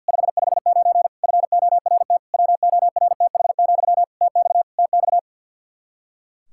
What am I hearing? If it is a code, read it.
Morse code, heard as a keyed tone, about 50 wpm, 700 Hz. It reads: 559 FORT WORTH, TX TX
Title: Practice QSO - Exchange 1 - Beginning - 50wpm